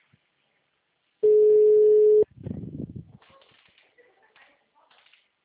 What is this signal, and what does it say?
Ringback tone (Europe, 425 Hz).
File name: out- -2001- - - .16699